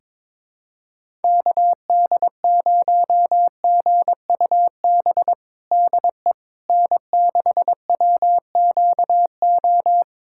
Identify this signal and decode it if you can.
Morse code, heard as a keyed tone, about 22 wpm, 700 Hz. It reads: KD0GUB DE N6WQO
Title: USA Callsign DE Callsign 22wpm